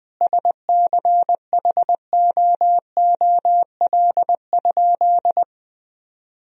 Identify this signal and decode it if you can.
Morse code, heard as a keyed tone, about 20 wpm, 700 Hz. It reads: SCHOOL?
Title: Sentences from Top 1000 Words - Review Part 3 - 20wpm